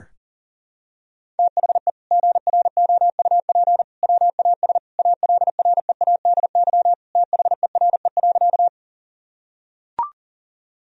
Morse code, heard as a keyed tone, about 40 wpm, 700 Hz. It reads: THE GROUP WAS ALREADY THERE.
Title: Sentences from Top 1000 Words - Review Part 1 - 40wpm